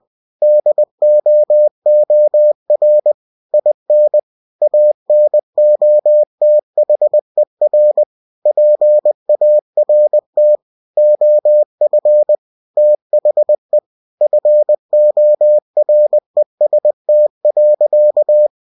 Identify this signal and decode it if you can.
Morse code, heard as a keyed tone, about 20 wpm, 600 Hz. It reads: DOOR IN ANOTHER PART OF THE FOREST.
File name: cw-books / winnie-the-pooh / winnie-20wpm-page-0004